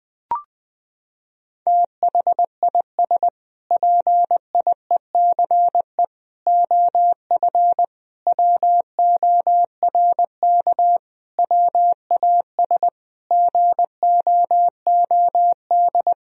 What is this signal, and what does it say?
Morse code, heard as a keyed tone, about 20 wpm, 700 Hz. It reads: THIS PIECE OF WORK WAS GOOD
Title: Sentences from Top 1000 Words - Review Part 2 - 20wpm